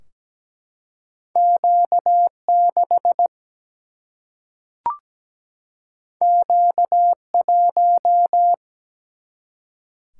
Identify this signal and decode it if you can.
Morse code, heard as a keyed tone, about 17 wpm, 700 Hz. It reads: Q6 Q1
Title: Single-Letter Single-Number 17wpm